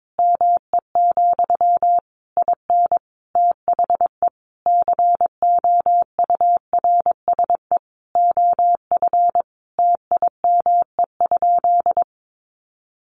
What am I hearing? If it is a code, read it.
Morse code, heard as a keyed tone, about 22 wpm, 700 Hz. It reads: ME, IN THE COURSE OF TIME?
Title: Sentences from Top 1200 Words 22wpm